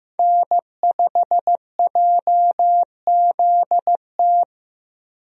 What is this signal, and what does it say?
Morse code, heard as a keyed tone, about 15 wpm, 700 Hz. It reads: N5JZT
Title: USA Call Signs - 5 Characters - Repeated 2x - 15wpm